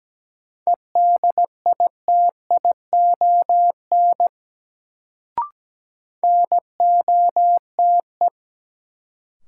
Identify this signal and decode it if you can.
Morse code, heard as a keyed tone, about 17 wpm, 700 Hz. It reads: EDITION NOTE
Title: Sets of 5 Words - New Word Drill - 17wpm